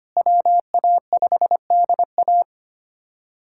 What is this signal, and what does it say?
Morse code, heard as a keyed tone, about 25 wpm, 700 Hz. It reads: WA5DA
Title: USA Callsign DE Callsign - 5 Characters - 25wpm